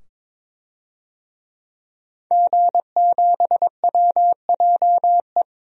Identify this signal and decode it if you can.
Morse code, heard as a keyed tone, about 22 wpm, 700 Hz. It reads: G7WJE
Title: DXCC-100 Call Signs - 5 Characters - 22wpm